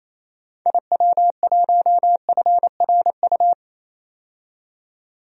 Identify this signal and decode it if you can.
Morse code, heard as a keyed tone, about 28 wpm, 700 Hz. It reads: IW1FRU
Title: DXCC-100 Call Signs - 6 Characters - 28wpm